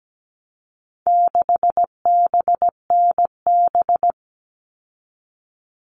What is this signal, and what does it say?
Morse code, heard as a keyed tone, about 17 wpm, 700 Hz. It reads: 6BNB
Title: All Cut Numbers - 4-Digit 17wpm